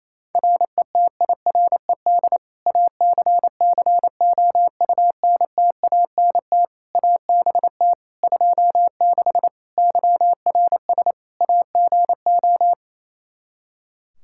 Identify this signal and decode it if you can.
Morse code, heard as a keyed tone, about 28 wpm, 700 Hz. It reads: RETIRED ACCOUNTANT ABT 26 YRS AGO
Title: Practice QSO - Exchange 3 - More Info - 28wpm